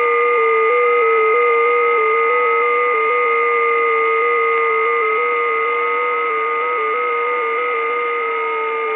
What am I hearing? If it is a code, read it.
FT8.
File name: test / websdr test14